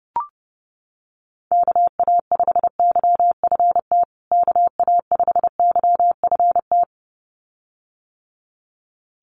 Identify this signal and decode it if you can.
Morse code, heard as a keyed tone, about 30 wpm, 700 Hz. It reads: KA5YFT KA5YFT
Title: USA Call Signs Repeated 2x 30wpm